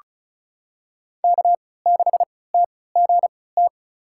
Morse code, heard as a keyed tone, about 36 wpm, spaced as for 20 wpm, 700 Hz. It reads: K6TGT
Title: USA Call Signs - 5 Characters - 3x Character Spacing - 35wpm